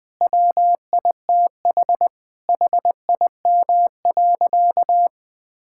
Morse code, heard as a keyed tone, about 20 wpm, 700 Hz. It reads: WITH HIM.
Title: Sentences from Top 1000 Words - Review Part 6 - 20wpm